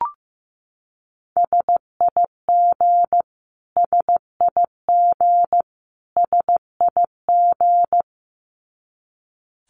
Morse code, heard as a keyed tone, about 15 wpm, 700 Hz. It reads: SIG SIG SIG
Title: Abbreviations Repeated 3x - 15wpm